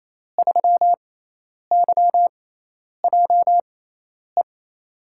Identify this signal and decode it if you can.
Morse code, heard as a keyed tone, about 28 wpm, spaced as for 10 wpm, 700 Hz. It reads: 3YJE
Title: USA Call Signs - 5 Characters - 6x Character Spacing - 28wpm